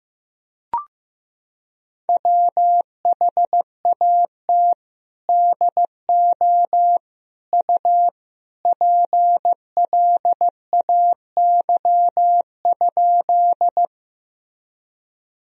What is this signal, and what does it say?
Morse code, heard as a keyed tone, about 15 wpm, 700 Hz. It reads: WHAT DO U PLAY?
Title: Sentences from Top 200 Words and Abbreviations 15wpm